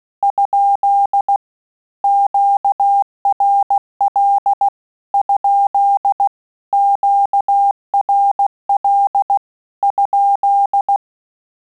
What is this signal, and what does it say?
Morse code, heard as a keyed tone, about 16 wpm, 800 Hz. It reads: ? QRL ? QRL ?
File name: Telegraphy-group / QSO CW 15 PPM